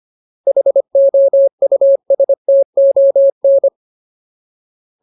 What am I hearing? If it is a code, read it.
Morse code, heard as a keyed tone, about 25 wpm, 550 Hz. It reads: HOUSTON
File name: Ultimate Word List / 25WPM Ultimate